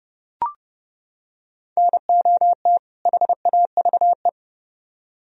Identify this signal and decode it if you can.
Morse code, heard as a keyed tone, about 30 wpm, 700 Hz. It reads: NOT HAVE